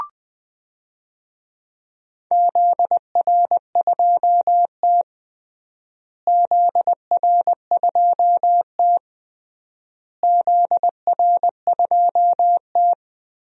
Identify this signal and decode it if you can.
Morse code, heard as a keyed tone, about 20 wpm, 700 Hz. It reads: ZR2T ZR2T ZR2T
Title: DXCC-100 Call Signs - 4 Characters - Repeated 3x - 20wpm